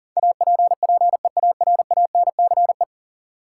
Morse code, heard as a keyed tone, about 40 wpm, 700 Hz. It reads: APPEARANCE
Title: Sentences from Top 1100 Words - New Word Drill Alt - 40wpm